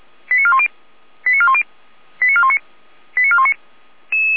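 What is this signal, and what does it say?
Five-tone selective call (ZVEI), heard as quick runs of five tones, tones of about 70 ms. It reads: 89410 89410 89419 89419
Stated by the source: ZVEI five-tone selective call received by SDR.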